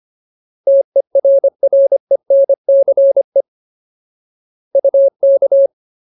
Morse code, heard as a keyed tone, about 25 wpm, 550 Hz. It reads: TERRENCE UK
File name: Ultimate Word List / 25WPM Ultimate